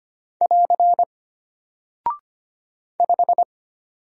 Morse code, heard as a keyed tone, about 25 wpm, 700 Hz. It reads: <AR> 5
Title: Single Character (Letter:Number:Punctuation:Prosign) Alt 25wpm